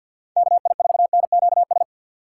Morse code, heard as a keyed tone, about 47 wpm, 700 Hz. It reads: KI4NQS
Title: USA Call Signs - 6x Character Spacing - 50wpm